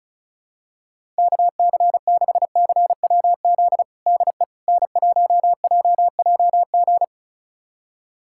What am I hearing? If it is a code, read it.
Morse code, heard as a keyed tone, about 36 wpm, 700 Hz. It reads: KC6CWZ DE N1JJG
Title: USA Callsign DE Callsign 35wpm